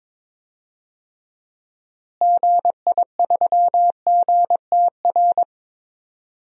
Morse code, heard as a keyed tone, about 22 wpm, 700 Hz. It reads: GI3GTR
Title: DXCC-100 Call Signs - 6 Characters - Repeated 2x - 22wpm